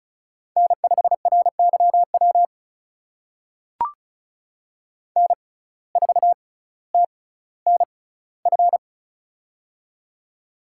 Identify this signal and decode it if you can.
Morse code, heard as a keyed tone, about 36 wpm, spaced as for 18 wpm, 700 Hz. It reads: N5RYW N4TNF
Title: USA Call Signs - 5 Characters - 6x Character Spacing - 35wpm